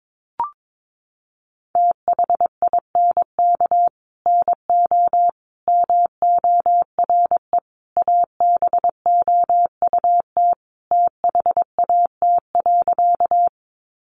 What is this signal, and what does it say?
Morse code, heard as a keyed tone, about 22 wpm, 700 Hz. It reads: THINK NO MORE ABOUT THAT.
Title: Sentences from Top 200 Words 22wpm